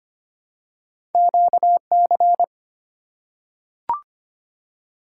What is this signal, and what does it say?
Morse code, heard as a keyed tone, about 25 wpm, 700 Hz. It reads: QC
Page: morse-code-ninja-practice-25wpm